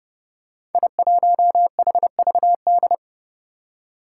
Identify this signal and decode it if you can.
Morse code, heard as a keyed tone, about 30 wpm, 700 Hz. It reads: I1HVD
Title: Five Challenging Letter-Number Combinations 30wpm